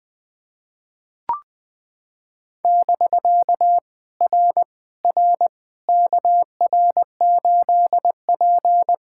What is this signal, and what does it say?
Morse code, heard as a keyed tone, about 20 wpm, 700 Hz. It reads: <BK> R R KR8P
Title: POTA QSO Practice - 20wpm